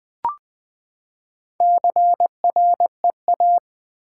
Morse code, heard as a keyed tone, about 20 wpm, 700 Hz. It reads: CREA
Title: Build A Word - 4 Letters At A Time - 8-Letter Words - 20wpm